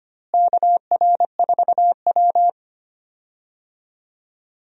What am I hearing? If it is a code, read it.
Morse code, heard as a keyed tone, about 25 wpm, 700 Hz. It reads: KR4W
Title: USA Call Signs - 4 Characters - Repeated 3x - 25wpm